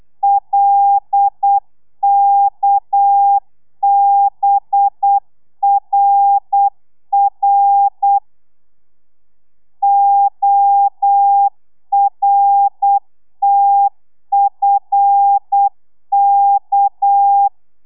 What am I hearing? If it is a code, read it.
Morse code, heard as a keyed tone, about 8 wpm, 800 Hz. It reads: LKBRR ORTFK
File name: telegrafia / i2tfj / Lez6 35